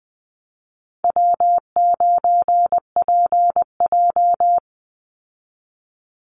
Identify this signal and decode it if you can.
Morse code, heard as a keyed tone, about 19 wpm, 700 Hz. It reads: W9PJ
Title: Easy Call Signs Repeated 3x - 20wpm